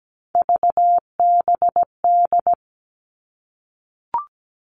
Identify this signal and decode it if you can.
Morse code, heard as a keyed tone, about 17 wpm, 700 Hz. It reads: VBD
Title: All Cut Numbers - 3-Digit 17wpm